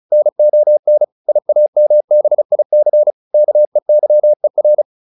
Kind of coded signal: Morse code

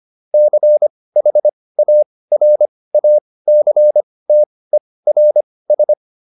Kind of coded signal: Morse code

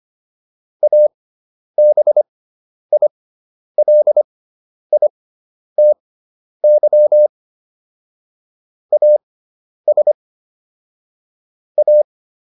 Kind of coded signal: Morse code

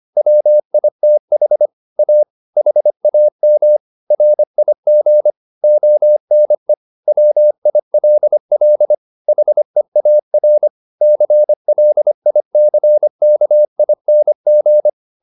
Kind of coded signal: Morse code

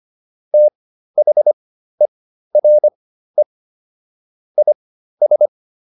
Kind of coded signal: Morse code

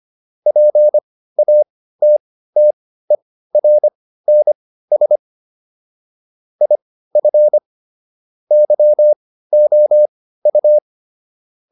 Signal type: Morse code

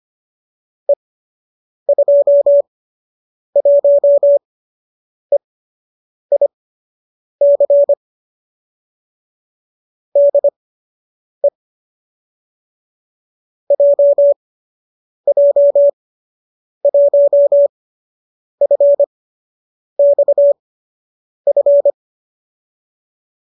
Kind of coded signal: Morse code

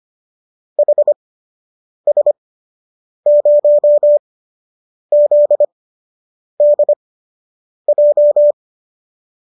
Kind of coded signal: Morse code

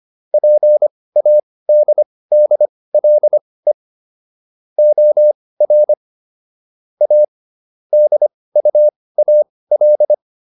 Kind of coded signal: Morse code

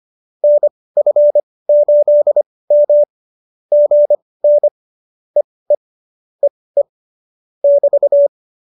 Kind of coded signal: Morse code

